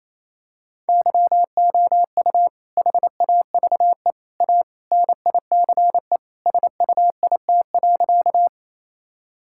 Morse code, heard as a keyed tone, about 28 wpm, 700 Hz. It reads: YOU HAVE A NICE SUIT.
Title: Sentences from Top 1600 Words 28wpm